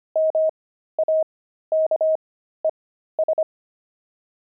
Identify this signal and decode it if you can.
Morse code, heard as a keyed tone, about 25 wpm, spaced as for 13 wpm, 630 Hz. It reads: MAKES
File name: pangram-fs-13-cs-25-oxide